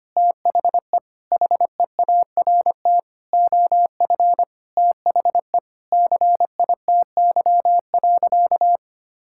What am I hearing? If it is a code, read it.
Morse code, heard as a keyed tone, about 25 wpm, 700 Hz. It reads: THE HEART OF THE CITY.
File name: practice / 25wpm / Sentences with 4-Word Phrases - 25wpm